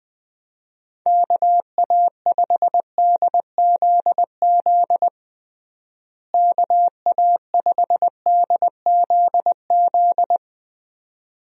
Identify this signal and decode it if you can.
Morse code, heard as a keyed tone, about 20 wpm, 700 Hz. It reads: KA5DZZ KA5DZZ
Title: USA Call Signs - 6 Characters - Repeated 2x - 20wpm